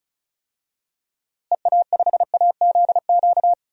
Morse code, heard as a keyed tone, about 36 wpm, 700 Hz. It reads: EA5AZQ
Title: DXCC-100 Call Signs - 6 Characters - Repeated 3x - 35wpm